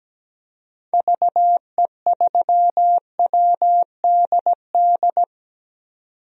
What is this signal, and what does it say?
Morse code, heard as a keyed tone, about 17 wpm, 700 Hz. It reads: VE3WDD
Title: DXCC-100 Call Signs - 6 Characters - 17wpm